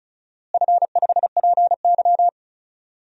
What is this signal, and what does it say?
Morse code, heard as a keyed tone, about 36 wpm, 700 Hz. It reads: F5PY
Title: DXCC-100 Call Signs - 4 Characters - Repeated 3x - 35wpm